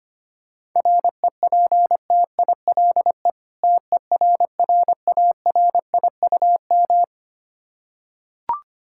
Morse code, heard as a keyed tone, about 25 wpm, 700 Hz. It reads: REPTILE TERRARIUM
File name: practice / 25wpm / Difficult Words with Hints - 25wpm